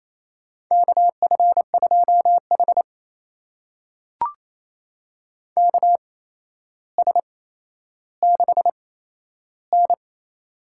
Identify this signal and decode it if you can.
Morse code, heard as a keyed tone, about 27 wpm, spaced as for 12 wpm, 700 Hz. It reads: KF2H KS6N
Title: USA Call Signs - 4 Characters - 8x Character Spacing - 28wpm